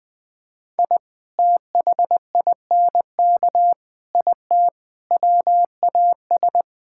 Morse code, heard as a keyed tone, about 21 wpm, 700 Hz. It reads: I THINK IT WAS
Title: Sentences with 4-Word Phrases - Spoken Beginning - 20wpm